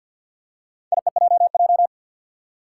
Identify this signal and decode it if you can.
Morse code, heard as a keyed tone, about 52 wpm, 700 Hz. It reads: IEJJ
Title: Four Challenging Letter-Number Combinations 50wpm